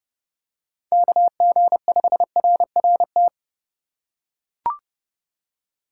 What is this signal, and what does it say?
Morse code, heard as a keyed tone, about 28 wpm, 700 Hz. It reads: KG5RRT